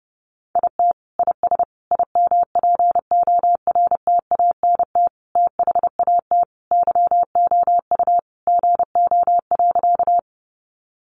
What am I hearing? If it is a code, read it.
Morse code, heard as a keyed tone, about 30 wpm, 700 Hz. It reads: IT IS IMPORTANT THAT YOU GO.
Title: Sentences from Top 600 Words 30wpm